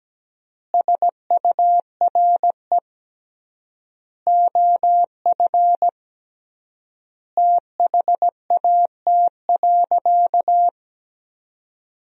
Morse code, heard as a keyed tone, about 17 wpm, 700 Hz. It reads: SURE OF THAT.
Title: 3x Word-Spacing - Sentences from Top 300 Words - 17wpm